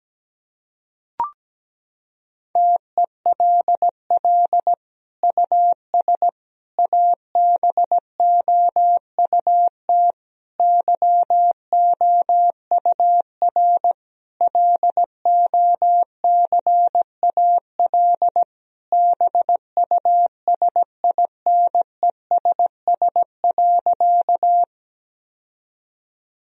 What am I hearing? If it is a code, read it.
Morse code, heard as a keyed tone, about 17 wpm, 700 Hz. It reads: TELL US ABOUT YOUR LOCAL BUSINESS.
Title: Sentences from Top 1000 Words - Review Part 3 - 17wpm